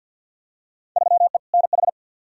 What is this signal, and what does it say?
Morse code, heard as a keyed tone, about 47 wpm, 700 Hz. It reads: 3E NH
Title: Field Day Exchanges Encore - 50wpm